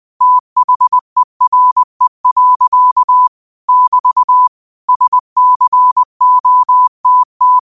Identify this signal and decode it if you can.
Morse code, heard as a keyed tone, about 20 wpm, 1000 Hz. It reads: THERE. <BT> SCOTT